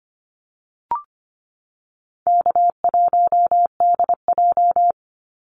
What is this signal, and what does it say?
Morse code, heard as a keyed tone, about 25 wpm, 700 Hz. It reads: K1DJ
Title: Easy Call Signs - 25wpm